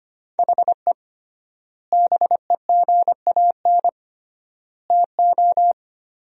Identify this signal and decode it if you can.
Morse code, heard as a keyed tone, about 25 wpm, 700 Hz. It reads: HE BEGAN TO
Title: 3x Word-Spacing - Sentences from Top 500 Words - 25wpm